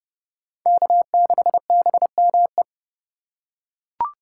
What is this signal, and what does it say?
Morse code, heard as a keyed tone, about 30 wpm, 700 Hz. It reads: K6BME
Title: USA Call Signs - 5 Characters - 4x Character Spacing - 30wpm